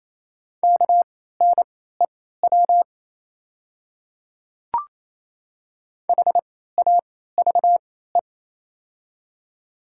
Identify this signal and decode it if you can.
Morse code, heard as a keyed tone, about 28 wpm, spaced as for 16 wpm, 700 Hz. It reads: KNEW HAVE
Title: 3x Character Spacing - Top 500 Words - 28wpm